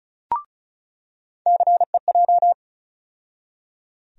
Morse code, heard as a keyed tone, about 34 wpm, 700 Hz. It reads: CEJ
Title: USA Call Sign Postfixes - 3 Letters 35wpm